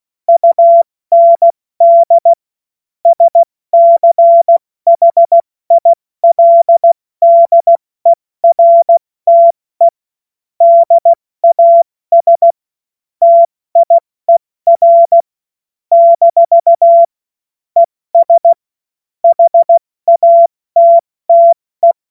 Morse code, heard as a keyed tone, about 16 wpm, 680 Hz. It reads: UND SCHILDERTE DAS TIER - ES HATTE
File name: Morsekurs / Witze / Witz 07. März 2023 70 F80